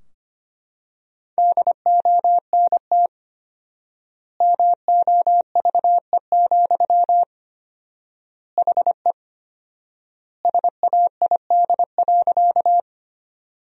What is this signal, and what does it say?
Morse code, heard as a keyed tone, about 25 wpm, 700 Hz. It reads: DONT MOVE, HE SAID.
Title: 4x Word-Spacing - Sentences from Top 300 Words - 25wpm